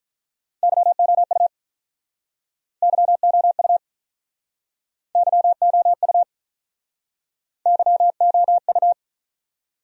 Morse code, heard as a keyed tone, about 48 wpm, 700 Hz. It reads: YOU YOU YOU YOU